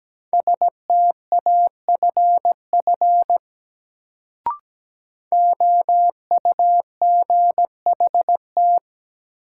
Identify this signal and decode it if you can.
Morse code, heard as a keyed tone, about 17 wpm, 700 Hz. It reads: STAFF OUGHT
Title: Sets of 5 Words Encore - New Word Drill - 17wpm